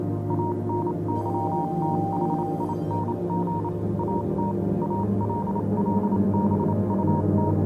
Morse code, heard as a keyed tone, about 28 wpm, spaced as for 25 wpm, 940 Hz, with noise. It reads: ANYTHING AT ALL BUT
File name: Vylet Pony / aura / do you remember the song she sang (morse code version)